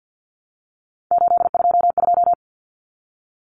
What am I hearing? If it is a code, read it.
Morse code, heard as a keyed tone, about 50 wpm, 700 Hz. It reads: 822
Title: Common Cut Numbers - 3-Digit 50wpm